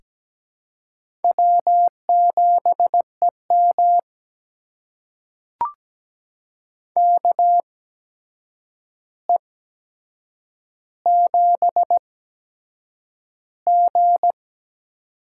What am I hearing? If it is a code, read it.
Morse code, heard as a keyed tone, about 17 wpm, spaced as for 7 wpm, 700 Hz. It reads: W7EM KE7G
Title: USA Call Signs - 8x Character Spacing - 17wpm